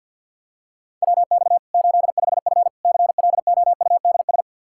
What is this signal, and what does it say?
Morse code, heard as a keyed tone, about 52 wpm, 700 Hz. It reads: WX 85F CLOUDS
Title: Practice QSO - Exchange 2 - Info - 50wpm